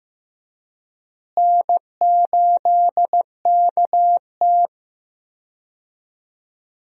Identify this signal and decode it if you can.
Morse code, heard as a keyed tone, about 15 wpm, 700 Hz. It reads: N8KT
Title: USA Call Signs - 4 Characters - Repeated 2x - 15wpm